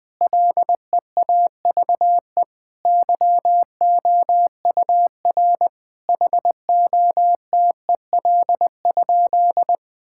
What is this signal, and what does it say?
Morse code, heard as a keyed tone, about 20 wpm, 700 Hz. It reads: LEAVE YOUR HOTEL?
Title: Sentences from Top 1600 Words 20wpm